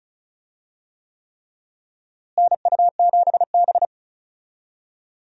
Morse code, heard as a keyed tone, about 36 wpm, 700 Hz. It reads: NU7B
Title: USA Call Signs - 4 Characters - Repeated 3x - 35wpm